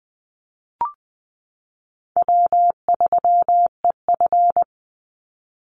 Morse code, heard as a keyed tone, about 20 wpm, 700 Hz. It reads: W3EF
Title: Easy Call Signs Repeated 3x - 20wpm